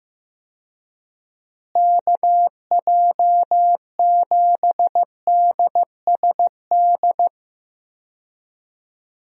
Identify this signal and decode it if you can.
Morse code, heard as a keyed tone, about 15 wpm, 700 Hz. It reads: KJ7DSD
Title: USA Call Signs - 6 Characters - Repeated 3x - 15wpm